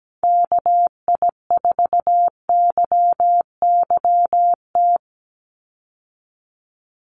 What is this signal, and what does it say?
Morse code, heard as a keyed tone, about 17 wpm, 700 Hz. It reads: KI4YYT
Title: USA Call Signs 17wpm